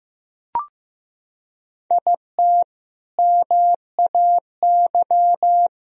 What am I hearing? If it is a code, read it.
Morse code, heard as a keyed tone, about 15 wpm, 700 Hz. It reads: IT MAY
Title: Sentences from Top 400 Words and Abbreviations 15wpm